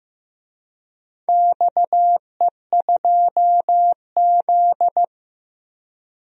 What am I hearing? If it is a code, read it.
Morse code, heard as a keyed tone, about 15 wpm, 700 Hz. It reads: XE2Z
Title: DXCC-100 Call Signs - 4 Characters - 15wpm